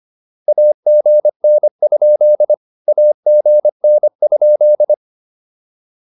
Morse code, heard as a keyed tone, about 25 wpm, 600 Hz. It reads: AGN? AGN?